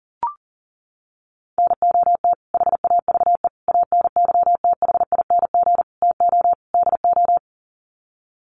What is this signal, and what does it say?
Morse code, heard as a keyed tone, about 40 wpm, 700 Hz. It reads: NOT HAVE ANYTHING TO DO
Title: Sets of 5 Words Encore 40wpm